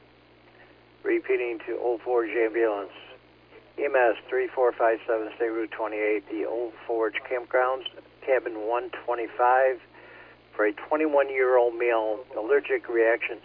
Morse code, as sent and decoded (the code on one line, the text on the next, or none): none